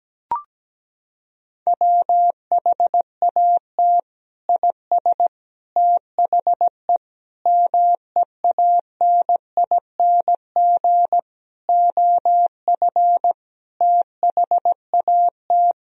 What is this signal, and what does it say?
Morse code, heard as a keyed tone, about 17 wpm, 700 Hz. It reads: WHAT IS THE MEANING OF THAT
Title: Sentences from Top 1000 Words - Review Part 4 - 17wpm